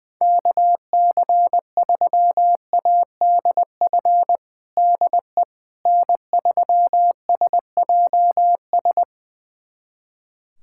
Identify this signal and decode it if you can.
Morse code, heard as a keyed tone, about 20 wpm, 700 Hz. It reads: KC3ADF DE N3SJS
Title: USA Callsign DE Callsign 20wpm